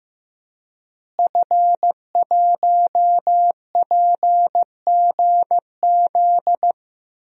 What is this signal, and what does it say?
Morse code, heard as a keyed tone, about 15 wpm, 700 Hz. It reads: F1PGZ
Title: DXCC-100 Call Signs - 5 Characters - Repeated 2x - 15wpm